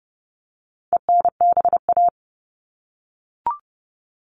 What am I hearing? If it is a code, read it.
Morse code, heard as a keyed tone, about 30 wpm, 700 Hz. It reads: ENBA
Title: All Cut Numbers - 4-Digit 30wpm